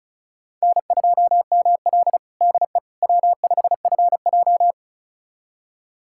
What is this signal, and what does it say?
Morse code, heard as a keyed tone, about 36 wpm, 700 Hz. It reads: N2ML DE W5FJ
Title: USA Callsign DE Callsign - 4 Characters - 35wpm